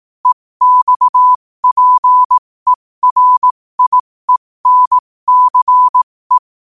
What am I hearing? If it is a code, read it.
Morse code, heard as a keyed tone, about 18 wpm, spaced as for 15 wpm, 1000 Hz. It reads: EXPERIENCE
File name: qotdfiles / 15wpm